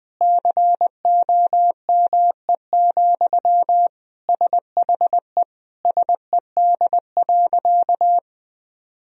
Morse code, heard as a keyed tone, about 20 wpm, 700 Hz. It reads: COME, SHE SED.
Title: Sentences from Top 100 Words and Abbreviations 20wpm